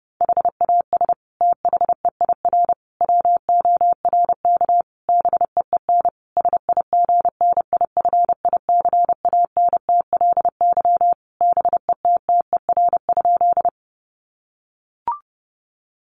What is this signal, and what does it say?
Morse code, heard as a keyed tone, about 30 wpm, 700 Hz. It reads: HAS THEIR WORK BEEN SIGNIFICANTLY BETTER?
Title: Sentences from Top 1500 Words 30wpm